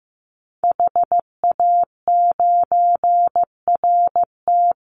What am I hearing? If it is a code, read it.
Morse code, heard as a keyed tone, about 15 wpm, 700 Hz. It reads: HA9RT
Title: Call Signs Repeated 3x - 15wpm